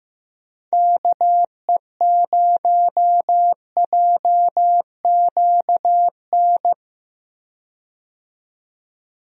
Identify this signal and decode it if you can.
Morse code, heard as a keyed tone, about 15 wpm, 700 Hz. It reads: KE0JQN
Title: USA Call Signs - 6 Characters - 15wpm